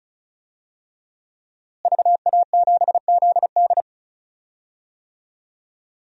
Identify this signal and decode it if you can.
Morse code, heard as a keyed tone, about 36 wpm, 700 Hz. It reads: VA7ZD
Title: DXCC-100 Call Signs - 5 Characters - Repeated 3x - 35wpm